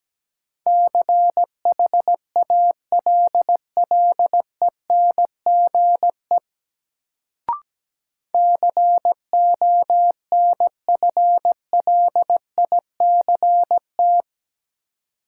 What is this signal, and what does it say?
Morse code, heard as a keyed tone, about 17 wpm, 700 Hz. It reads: CHALLENGE CONFLICT
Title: Sentences from Top 1300 Words - New Word Drill Alt - 17wpm